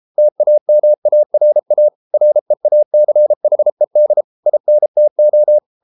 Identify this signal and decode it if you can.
Morse code, heard as a keyed tone, about 33 wpm, 600 Hz. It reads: TAMARA REACHED INTO